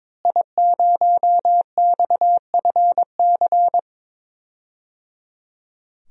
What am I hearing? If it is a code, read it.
Morse code, heard as a keyed tone, about 21 wpm, 700 Hz. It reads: I0XFC
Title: DXCC-100 Call Signs - 5 Characters - Repeated 3x - 22wpm